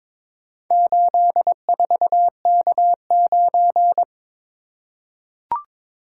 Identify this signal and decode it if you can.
Morse code, heard as a keyed tone, about 21 wpm, 700 Hz. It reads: 84K9
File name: practice / 22wpm / Four Letter-Number Combinations - 22wpm